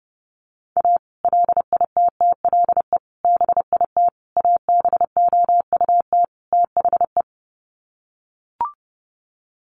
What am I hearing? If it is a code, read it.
Morse code, heard as a keyed tone, about 30 wpm, 700 Hz. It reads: A LITTLE BIT ABOUT THE
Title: Sets of 5 Words Encore2 30wpm